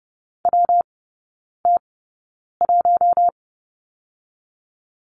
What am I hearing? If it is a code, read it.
Morse code, heard as a keyed tone, about 30 wpm, 700 Hz. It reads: W T 1
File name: Lesson 19e - Letter w - Three Characters 30wpm